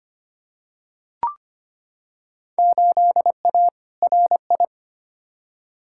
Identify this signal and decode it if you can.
Morse code, heard as a keyed tone, about 25 wpm, 700 Hz. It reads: 8A RI
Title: Field Day Exchanges Encore - 25wpm